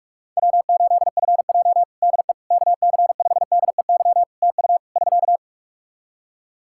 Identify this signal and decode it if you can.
Morse code, heard as a keyed tone, about 46 wpm, 700 Hz. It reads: W8FJ DE KC5DEY TU <SK>
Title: Practice QSO - Exchange 4 - Ending - 45wpm